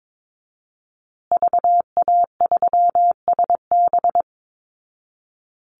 Morse code, heard as a keyed tone, about 22 wpm, 700 Hz. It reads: VA3SB
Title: Call Signs Repeated 2x - 22wpm